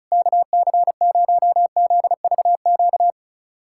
Morse code, heard as a keyed tone, about 34 wpm, 700 Hz. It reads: KC0ZVQ
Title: USA Callsign DE Callsign 35wpm